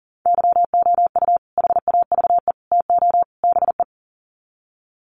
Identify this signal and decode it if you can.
Morse code, heard as a keyed tone, about 40 wpm, 700 Hz. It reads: YOU HAVE TO BE
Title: Sets of 4 Words 40wpm